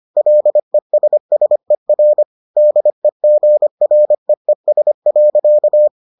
Morse code, heard as a keyed tone, about 25 wpm, 600 Hz. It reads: LESSER DEGREES.